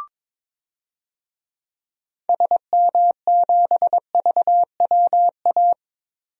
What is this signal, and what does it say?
Morse code, heard as a keyed tone, about 22 wpm, 700 Hz. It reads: SM7VWA
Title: DXCC-100 Call Signs - 6 Characters - 22wpm